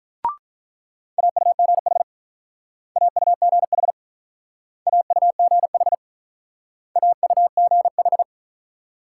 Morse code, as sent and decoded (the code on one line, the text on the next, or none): .- ..- --. .... / .- ..- --. .... / .- ..- --. .... / .- ..- --. ....
AUGH AUGH AUGH AUGH